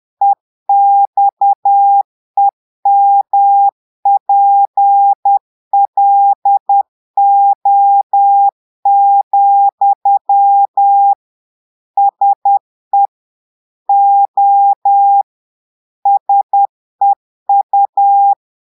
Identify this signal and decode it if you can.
Morse code, heard as a keyed tone, about 10 wpm, 800 Hz. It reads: EXEMPLO, SE O SEU
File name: portugais texte en clair 2 vitesse 10 mots minutes